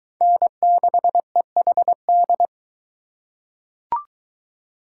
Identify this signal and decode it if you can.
Morse code, heard as a keyed tone, about 22 wpm, 700 Hz. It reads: N6EHD